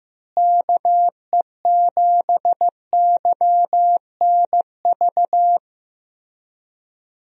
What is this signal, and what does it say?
Morse code, heard as a keyed tone, about 15 wpm, 700 Hz. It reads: KE7YNV
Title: USA Call Signs - 6 Characters - Repeated 3x - 15wpm